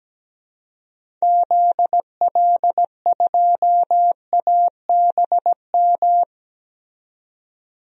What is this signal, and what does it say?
Morse code, heard as a keyed tone, about 17 wpm, 700 Hz. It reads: ZL2ABM
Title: DXCC-100 Call Signs - 6 Characters - 17wpm